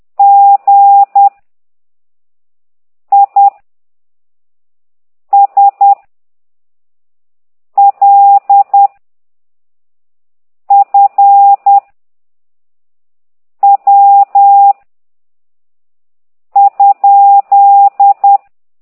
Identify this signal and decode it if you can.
Morse code, heard as a keyed tone, about 10 wpm, 800 Hz. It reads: G I S L F W ?
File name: ct2idw / exerc11